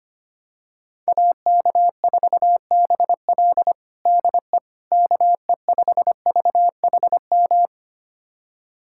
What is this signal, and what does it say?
Morse code, heard as a keyed tone, about 25 wpm, 700 Hz. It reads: AK4BL DE KE5VHM
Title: USA Callsign DE Callsign 25wpm